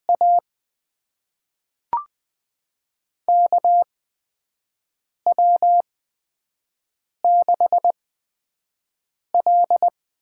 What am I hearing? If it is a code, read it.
Morse code, heard as a keyed tone, about 20 wpm, spaced as for 5 wpm, 700 Hz. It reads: A KW6L
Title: USA Call Signs - 4 Characters - 8x Character Spacing - 20wpm